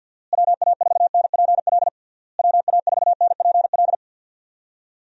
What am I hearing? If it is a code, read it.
Morse code, heard as a keyed tone, about 46 wpm, 700 Hz. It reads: WA4NPL WA4NPL
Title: USA Call Signs - 6 Characters - Repeated 3x - 50wpm